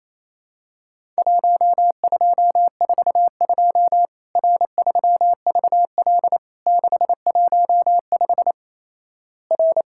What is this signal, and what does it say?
Morse code, heard as a keyed tone, about 28 wpm, 700 Hz. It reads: 1242 R3VL 615 R
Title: Worked All Europe DX Contest - QTC Exchange - 28wpm